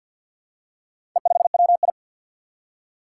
Morse code, heard as a keyed tone, about 51 wpm, 700 Hz. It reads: E5WI
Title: Four Challenging Letter-Number Combinations 50wpm